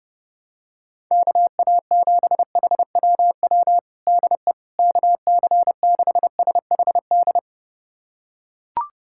Morse code, heard as a keyed tone, about 30 wpm, 700 Hz. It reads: KA7HWW DE KC6SHD
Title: USA Callsign DE Callsign 30wpm